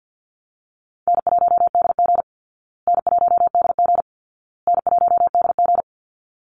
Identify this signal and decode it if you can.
Morse code, heard as a keyed tone, about 50 wpm, 700 Hz. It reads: N1DG N1DG N1DG
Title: Call Signs Repeated 2x - 50wpm